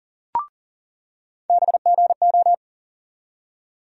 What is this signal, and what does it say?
Morse code, heard as a keyed tone, about 40 wpm, 700 Hz. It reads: BGO
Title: USA Call Sign Postfixes - 3 Letters 40wpm